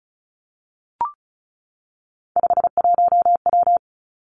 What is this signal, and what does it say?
Morse code, heard as a keyed tone, about 36 wpm, 700 Hz. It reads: 51W
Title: Three Challenging Letter-Number Combinations 35wpm